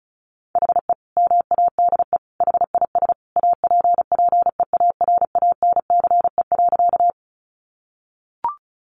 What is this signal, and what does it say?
Morse code, heard as a keyed tone, about 34 wpm, 700 Hz. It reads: HE MADE HIS APPEARANCE.
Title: Sentences from Top 1100 Words 35wpm